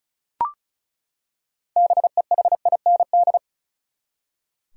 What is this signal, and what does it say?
Morse code, heard as a keyed tone, about 34 wpm, 700 Hz. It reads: BEHIND
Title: Six Letter Words - 35wpm